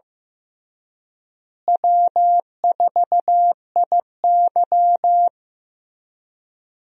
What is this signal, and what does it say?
Morse code, heard as a keyed tone, about 15 wpm, 700 Hz. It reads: W4IY
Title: USA Call Signs - 4 Characters - Repeated 3x - 15wpm